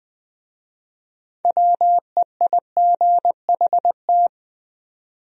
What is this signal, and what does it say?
Morse code, heard as a keyed tone, about 20 wpm, 700 Hz. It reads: WEIGHT
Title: Speed-Racing Top 500 Words 17wpm